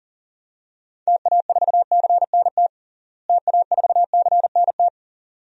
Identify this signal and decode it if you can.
Morse code, heard as a keyed tone, about 40 wpm, 700 Hz. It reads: TA4CNT TA4CNT
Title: DXCC-100 Call Signs - 6 Characters - Repeated 2x - 40wpm